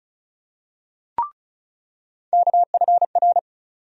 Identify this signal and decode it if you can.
Morse code, heard as a keyed tone, about 36 wpm, 700 Hz. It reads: KFR
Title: USA Call Sign Postfixes - 3 Letters 35wpm